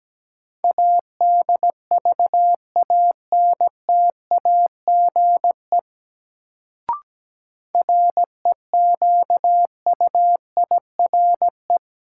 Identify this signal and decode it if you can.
Morse code, heard as a keyed tone, about 17 wpm, 700 Hz. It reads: ADVANTAGE REQUIRE
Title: Sentences from Top 900 Words - New Word Drill - 17wpm